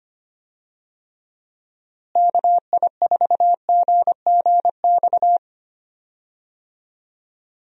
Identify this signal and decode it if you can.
Morse code, heard as a keyed tone, about 25 wpm, 700 Hz. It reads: KI4GGX
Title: USA Call Signs - 6 Characters - Repeated 3x - 25wpm